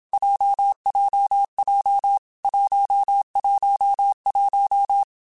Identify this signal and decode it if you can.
Morse code, heard as a keyed tone, about 27 wpm, 800 Hz. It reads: JJJ 111
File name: Groups@25 / Groups@25 mp3s / JZ17@25